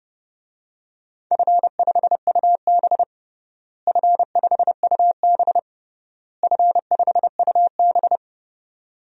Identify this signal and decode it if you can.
Morse code, heard as a keyed tone, about 30 wpm, 700 Hz. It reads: F5UB F5UB F5UB
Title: DXCC-100 Call Signs - 4 Characters - Repeated 3x - 30wpm